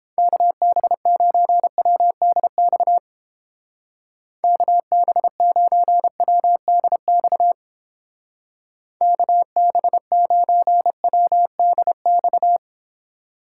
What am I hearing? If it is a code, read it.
Morse code, heard as a keyed tone, about 29 wpm, 700 Hz. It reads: KB9WDX KB9WDX KB9WDX